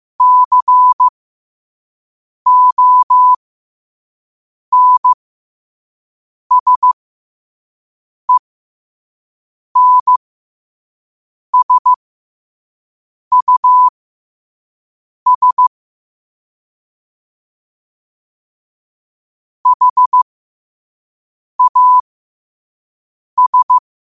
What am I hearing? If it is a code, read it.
Morse code, heard as a keyed tone, about 15 wpm, spaced as for 5 wpm, 1000 Hz. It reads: CONSENSUS HAS
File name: qotdfiles / 5wpm